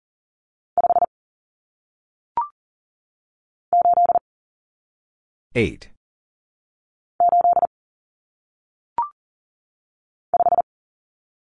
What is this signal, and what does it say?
Morse code, heard as a keyed tone, about 39 wpm, 700 Hz. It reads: 5 8 8 5
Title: Numbers - 1-Digit 40wpm